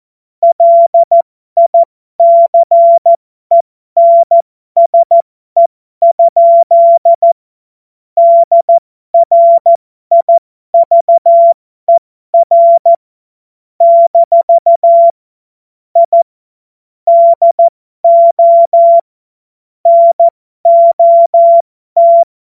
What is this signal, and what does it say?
Morse code, heard as a keyed tone, about 14 wpm, spaced as for 12 wpm, 680 Hz. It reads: LICENSE? DRIVER - I DO NOT
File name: Morsekurs / Witze